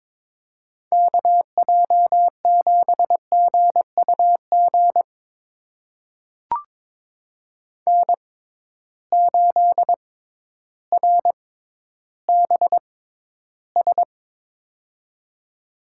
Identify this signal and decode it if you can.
Morse code, heard as a keyed tone, about 22 wpm, spaced as for 12 wpm, 700 Hz. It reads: KJ7GUG N8RBS
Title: USA Call Signs - 6x Character Spacing - 22wpm